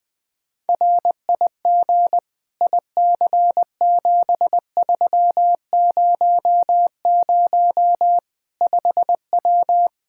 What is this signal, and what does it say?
Morse code, heard as a keyed tone, about 20 wpm, 700 Hz. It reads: RIG IC7300 5W
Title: Practice QSO - Exchange 2 - Info - 20wpm